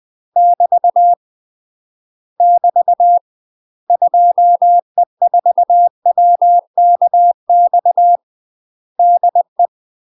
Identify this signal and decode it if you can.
Morse code, heard as a keyed tone, about 20 wpm, 700 Hz. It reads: <BT> <BT> 2E4WKX DE
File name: UK MASTER QSOs TRAINING / 20WPM / UK MASTER QSOs 0055